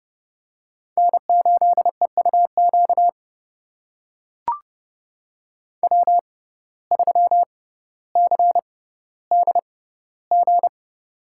Morse code, heard as a keyed tone, about 32 wpm, spaced as for 16 wpm, 700 Hz. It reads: N8EUQ W3CDG
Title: USA Call Signs - 5 Characters - 6x Character Spacing - 30wpm